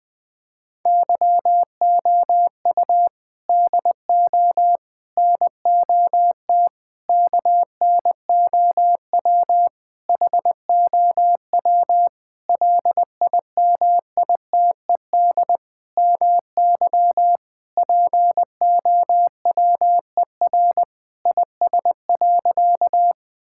Morse code, heard as a keyed tone, about 20 wpm, 700 Hz. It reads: YOU DO NOT KNOW HOW LIMITED MY POWER IS.
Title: Sentences from Top 1000 Words - Review Part 6 - 20wpm